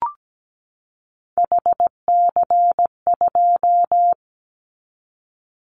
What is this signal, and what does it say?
Morse code, heard as a keyed tone, about 17 wpm, 700 Hz. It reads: HC2
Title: Callsign Prefixes for Top 100 DXCC - 17wpm